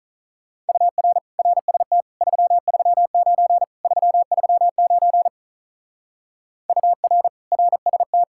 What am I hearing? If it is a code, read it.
Morse code, heard as a keyed tone, about 44 wpm, 700 Hz. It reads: UR RST 339 339 UR RST
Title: Speed-Racing Common QSO Phrases - 35wpm